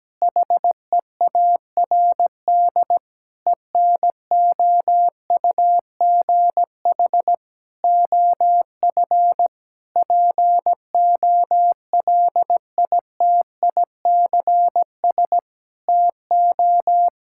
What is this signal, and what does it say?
Morse code, heard as a keyed tone, about 17 wpm, 700 Hz. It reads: HEARD ENOUGH OF POLITICS TO
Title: Sentences from Top 1000 Words - Review Part 6 - 17wpm